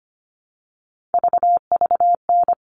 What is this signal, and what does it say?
Morse code, heard as a keyed tone, about 25 wpm, 700 Hz. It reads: VVN
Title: All Cut Numbers - 3-Digit 25wpm